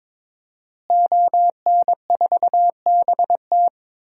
Morse code, heard as a keyed tone, about 21 wpm, 700 Hz. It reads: ON4BT
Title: DXCC-100 Call Signs - 5 Characters - Repeated 2x - 22wpm